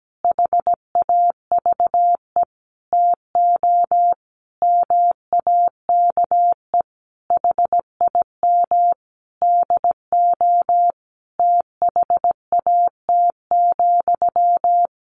Morse code, heard as a keyed tone, about 17 wpm, 700 Hz. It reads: HAVE TO MAKE HIM DO THAT,
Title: A Sound Of Thunder 17wpm